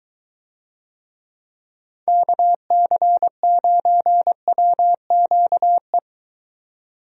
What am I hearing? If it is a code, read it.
Morse code, heard as a keyed tone, about 23 wpm, 700 Hz. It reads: KC9WQE